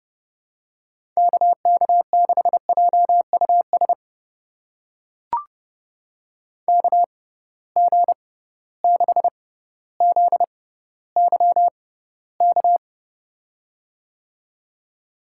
Morse code, heard as a keyed tone, about 30 wpm, spaced as for 15 wpm, 700 Hz. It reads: KK6JUS KG6ZYK
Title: USA Call Signs - 6 Characters - 6x Character Spacing - 30wpm